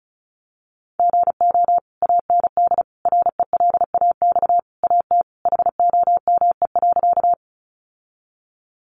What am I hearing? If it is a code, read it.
Morse code, heard as a keyed tone, about 34 wpm, 700 Hz. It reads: GO AND RELAX AT HOME.
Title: Sentences from Top 1400 Words 35wpm